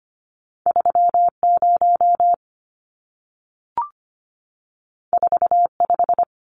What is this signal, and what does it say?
Morse code, heard as a keyed tone, about 25 wpm, 700 Hz. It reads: 30 45
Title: Numbers - 2-Digit 25wpm